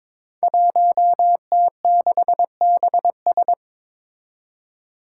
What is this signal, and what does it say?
Morse code, heard as a keyed tone, about 23 wpm, 700 Hz. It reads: 1T6BS
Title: Five Challenging Letter-Number Combinations 22wpm